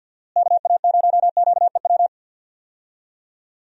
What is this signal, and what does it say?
Morse code, heard as a keyed tone, about 52 wpm, 700 Hz. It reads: KA0QEW
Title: USA Call Signs - 6 Characters - Repeated 2x - 50wpm